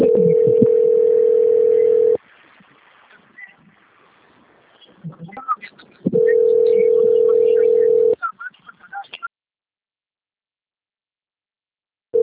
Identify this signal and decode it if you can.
Ringback tone (North America), 440 + 480 Hz.